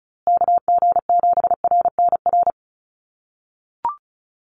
Morse code, heard as a keyed tone, about 34 wpm, 700 Hz. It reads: KG7RNR
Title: USA Call Signs Repeated 2x 35wpm